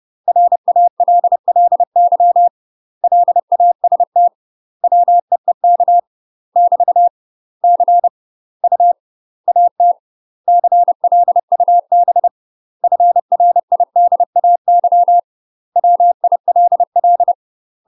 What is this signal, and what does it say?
Morse code, heard as a keyed tone, about 30 wpm, 700 Hz. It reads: RALLY LAST WEEK <BT> C U AT CLUB FRIDAY WILL
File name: UK MASTER QSOs TRAINING/30WPM